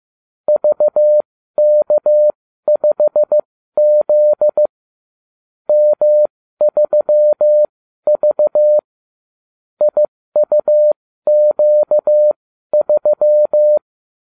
Morse code, heard as a keyed tone, about 15 wpm, spaced as for 12 wpm, 600 Hz. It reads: VK5Z M3V IUQ3